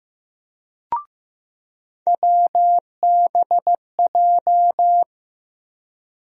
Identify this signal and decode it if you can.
Morse code, heard as a keyed tone, about 15 wpm, 700 Hz. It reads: WBJ
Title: USA Call Sign Postfixes - 3 Letters 15wpm